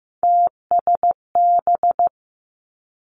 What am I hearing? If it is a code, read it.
Morse code, heard as a keyed tone, about 15 wpm, 700 Hz. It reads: TSB
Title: Three Challenging Letter-Number Combinations 15wpm